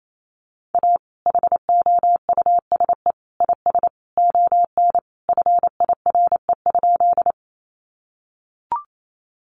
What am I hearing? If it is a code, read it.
Morse code, heard as a keyed tone, about 28 wpm, 700 Hz. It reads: A HOUSE IS ON FIRE?
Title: Sentences from Top 400 Words 28wpm